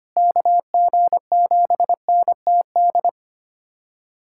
Morse code, heard as a keyed tone, about 25 wpm, 700 Hz. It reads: KG7NTD
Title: USA Callsign DE Callsign - 6 Characters - 25wpm